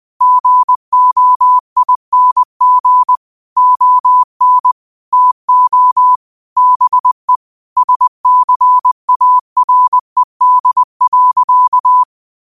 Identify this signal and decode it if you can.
Morse code, heard as a keyed tone, about 20 wpm, 1000 Hz. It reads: GOING ON TO BE SCARED.